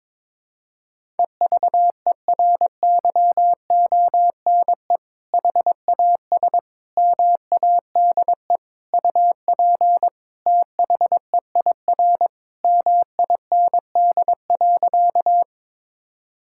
Morse code, heard as a keyed tone, about 22 wpm, 700 Hz. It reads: EVERYONE HAS MADE UP THEIR MIND.
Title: Sentences from Top 1000 Words - Review Part 5 - 22wpm